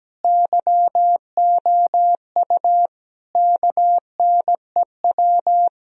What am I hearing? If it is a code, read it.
Morse code, heard as a keyed tone, about 17 wpm, 700 Hz. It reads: YOU KNEW
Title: Sentences from Top 1000 Words - Review Part 2 - 17wpm